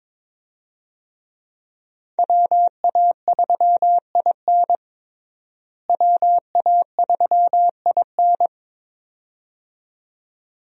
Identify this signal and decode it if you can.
Morse code, heard as a keyed tone, about 22 wpm, 700 Hz. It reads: WA3IN WA3IN
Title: USA Call Signs - 5 Characters - Repeated 2x - 22wpm